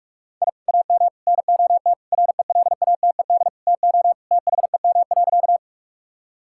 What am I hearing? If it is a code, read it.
Morse code, heard as a keyed tone, about 46 wpm, 700 Hz. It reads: I AM NOT RELATED TO THEM.
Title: Sentences from Top 1000 Words - Review Part 5 - 45wpm